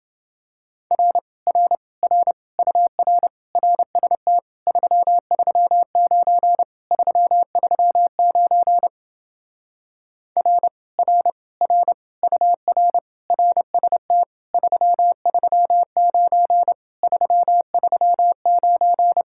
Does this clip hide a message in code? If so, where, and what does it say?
Yes: Morse code from 0.9 to 19.2 s, heard as a keyed tone, about 28 wpm, 700 Hz. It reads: R R R UR RST 339 339 R R R UR RST 339 339